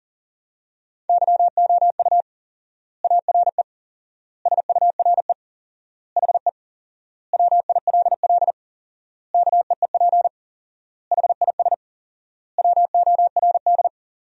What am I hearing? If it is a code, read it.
Morse code, heard as a keyed tone, about 37 wpm, 700 Hz. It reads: YOU ARE SURE HE WILL KEEP HIS WORD
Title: 4x Word-Spacing - Sentences from Top 300 Words - 40wpm